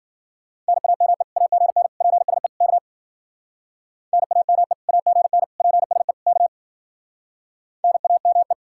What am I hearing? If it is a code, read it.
Morse code, heard as a keyed tone, about 55 wpm, 700 Hz. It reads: NAME AGN PSE K NAME AGN PSE K NAME